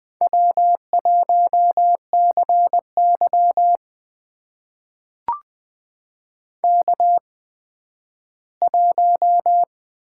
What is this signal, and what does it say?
Morse code, heard as a keyed tone, about 20 wpm, spaced as for 12 wpm, 700 Hz. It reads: W1CY K1
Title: USA Call Signs - 4 Characters - 8x Character Spacing - 20wpm